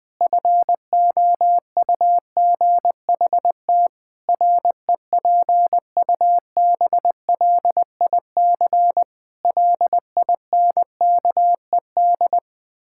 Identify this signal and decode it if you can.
Morse code, heard as a keyed tone, about 20 wpm, 700 Hz. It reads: FOUGHT REPUBLIC LINKED
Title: Pangram Challenge v2 - 20wpm